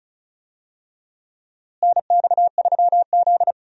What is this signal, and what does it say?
Morse code, heard as a keyed tone, about 34 wpm, 700 Hz. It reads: NX3Z
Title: USA Call Signs - 4 Characters - Repeated 3x - 35wpm